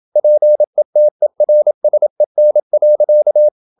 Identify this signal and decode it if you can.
Morse code, heard as a keyed tone, about 27 wpm, 600 Hz. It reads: PETERSEN.
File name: Ferie QRQ 27wpm